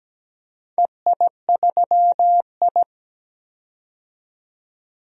Morse code, heard as a keyed tone, about 17 wpm, 700 Hz. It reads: EI3I
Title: DXCC-100 Call Signs - 4 Characters - 17wpm